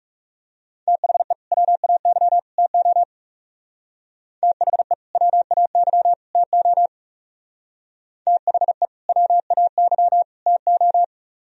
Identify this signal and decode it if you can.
Morse code, heard as a keyed tone, about 37 wpm, 700 Hz. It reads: THE WAY TO THE WAY TO THE WAY TO
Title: Speed-Racing - Sets of 3 Words 30wpm